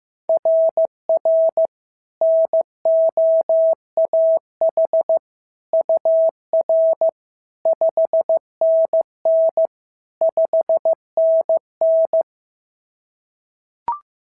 Morse code, heard as a keyed tone, about 15 wpm, 690 Hz. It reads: RR NOAH UR 5NN 5NN
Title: SOTA QSO Practice 15wpm